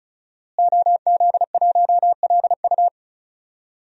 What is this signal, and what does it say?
Morse code, heard as a keyed tone, about 36 wpm, 700 Hz. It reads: OZ1LU
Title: DXCC-100 Call Signs - 5 Characters - Repeated 2x - 35wpm